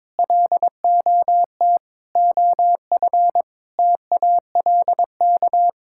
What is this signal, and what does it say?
Morse code, heard as a keyed tone, about 22 wpm, 700 Hz. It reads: LOT OF TALK
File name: practice / 22wpm / Sets of 8 Words - 22wpm